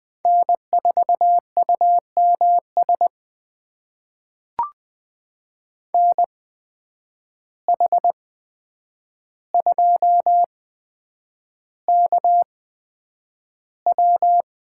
Morse code, heard as a keyed tone, about 20 wpm, spaced as for 9 wpm, 700 Hz. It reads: N4UMS NH2KW
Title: USA Call Signs - 5 Characters - 8x Character Spacing - 20wpm